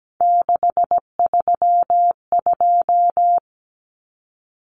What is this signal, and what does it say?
Morse code, heard as a keyed tone, about 17 wpm, 700 Hz. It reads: 632
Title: Numbers - 3-Digit 17wpm